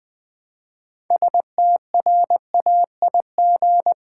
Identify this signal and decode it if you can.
Morse code, heard as a keyed tone, about 20 wpm, 700 Hz. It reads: STRAIG
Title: Build A Word - 3 Letters At A Time - 8-Letter Words - 20wpm